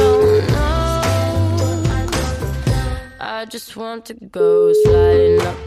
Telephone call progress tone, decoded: ringback tone

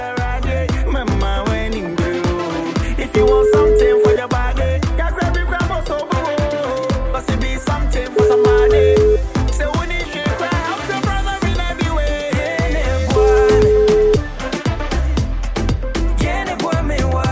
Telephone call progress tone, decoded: ringback tone

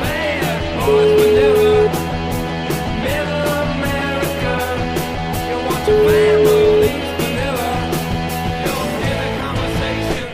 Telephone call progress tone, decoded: ringback tone

